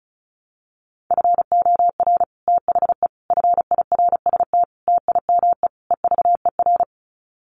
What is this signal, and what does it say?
Morse code, heard as a keyed tone, about 34 wpm, 700 Hz. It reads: FOR THE FIRST TIME EVER
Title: Sets of 5 Words Encore2 35wpm